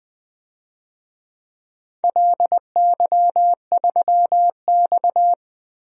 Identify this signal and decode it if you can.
Morse code, heard as a keyed tone, about 20 wpm, 700 Hz. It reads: LY3X
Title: DXCC-100 Call Signs - 4 Characters - Repeated 3x - 20wpm